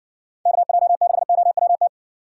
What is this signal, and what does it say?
Morse code, heard as a keyed tone, about 58 wpm, 700 Hz. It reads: KJ6OWT